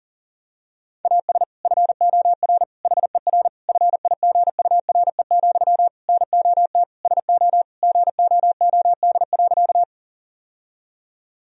Morse code, heard as a keyed tone, about 40 wpm, 700 Hz. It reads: AS FOR HER FIGURE, NOT SO GOOD.
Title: Sentences from Top 1000 Words - Review Part 2 - 40wpm